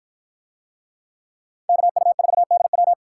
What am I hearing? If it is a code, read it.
Morse code, heard as a keyed tone, about 53 wpm, 700 Hz. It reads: KU4DW